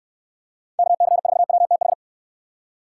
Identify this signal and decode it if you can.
Morse code, heard as a keyed tone, about 69 wpm, 700 Hz. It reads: KC4WTH